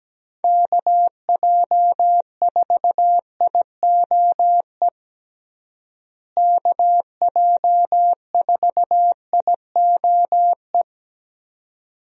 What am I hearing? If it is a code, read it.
Morse code, heard as a keyed tone, about 17 wpm, 700 Hz. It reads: KJ4IOE KJ4IOE
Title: USA Call Signs - 6 Characters - Repeated 2x - 17wpm